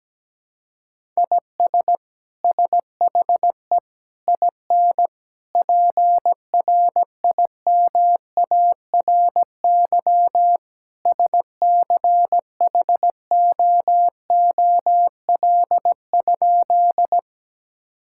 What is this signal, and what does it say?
Morse code, heard as a keyed tone, about 17 wpm, 700 Hz. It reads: IS SHE IN PRIMARY SCHOOL?
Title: Sentences from Top 1000 Words - Review Part 5 - 17wpm